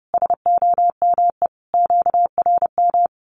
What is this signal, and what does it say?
Morse code, heard as a keyed tone, about 30 wpm, 700 Hz. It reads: SOME QRM
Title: Common QSO Phrases 30wpm